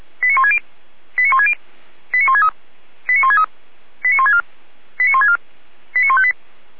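Five-tone selective call (ZVEI), heard as quick runs of five tones, tones of about 70 ms. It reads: 89160 89160 89163 89163 89164 89164 89158
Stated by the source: ZVEI five-tone call received by SDR.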